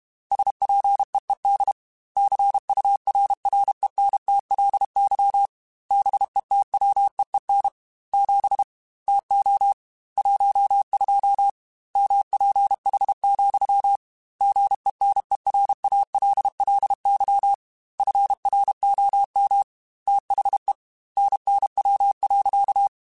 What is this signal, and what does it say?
Morse code, heard as a keyed tone, about 32 wpm, 800 Hz. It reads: SPEED CURRENTLY BETWEEN 7 TO 12 MPH, GENERALLY FROM THE NNW.
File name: Weather Sets / Weather Sets-mp3s / Weather 5 22 @30